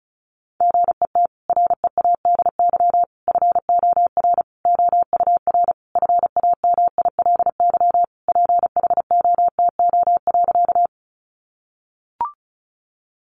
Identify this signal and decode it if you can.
Morse code, heard as a keyed tone, about 34 wpm, 700 Hz. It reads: GET READY FOR OUR FAMILY PHOTO.
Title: Sentences from Top 1200 Words 35wpm